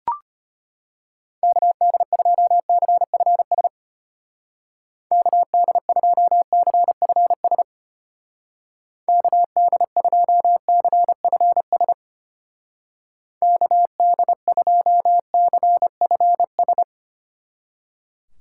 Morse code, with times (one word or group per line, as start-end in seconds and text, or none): 1.4-3.7 KD2CFS
5.1-7.6 KD2CFS
9.1-11.9 KD2CFS
13.4-16.8 KD2CFS